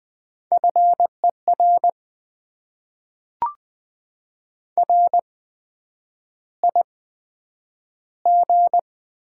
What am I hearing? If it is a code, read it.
Morse code, heard as a keyed tone, about 20 wpm, spaced as for 9 wpm, 700 Hz. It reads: FER RIG
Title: Top 20 QSO Elements - 8x Character Spacing - 20wpm